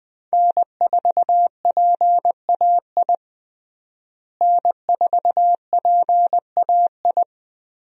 Morse code, heard as a keyed tone, about 20 wpm, 700 Hz. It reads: N4PAI N4PAI
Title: USA Call Signs - 5 Characters - Repeated 2x - 20wpm